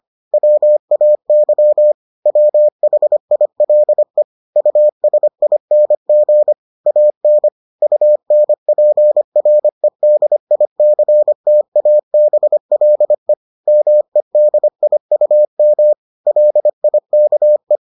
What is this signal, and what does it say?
Morse code, heard as a keyed tone, about 25 wpm, 600 Hz. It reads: WAY WHILE USING AN UNPREDICTABLE MEDIUM LIKE